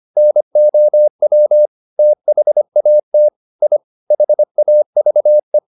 Morse code, heard as a keyed tone, about 25 wpm, 600 Hz. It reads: NOW THAT I HAVE